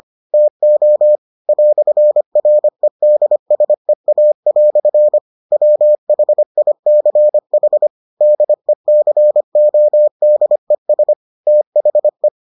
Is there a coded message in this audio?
Yes: Morse code from 0.3 to 12.3 s, heard as a keyed tone, about 25 wpm, 600 Hz. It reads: TO "REDSEA" WHICH DECODES THE